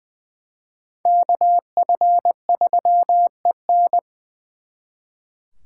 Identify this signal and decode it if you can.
Morse code, heard as a keyed tone, about 23 wpm, 700 Hz. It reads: KF3EN
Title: Speed Racing - USA 2x2 Call Signs 20wpm